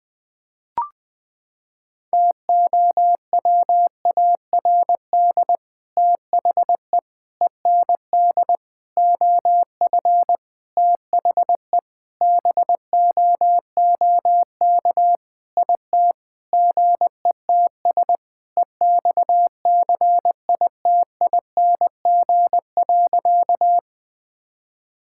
Morse code, heard as a keyed tone, about 20 wpm, 700 Hz. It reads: TOWARD THE END OF THE BOOK IT GETS EXCITING.
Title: Sentences with 5-Word Phrases - 20wpm